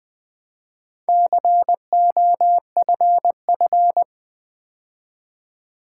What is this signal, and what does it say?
Morse code, heard as a keyed tone, about 20 wpm, 700 Hz. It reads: COFF
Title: Build A Word - 2 Letters At A Time - 6-Letter Words - 20wpm